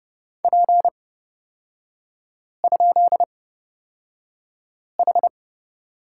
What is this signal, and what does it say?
Morse code, heard as a keyed tone, about 30 wpm, spaced as for 5 wpm, 700 Hz. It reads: P?H
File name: Lesson 38e - Letter q - Three Characters 30wpm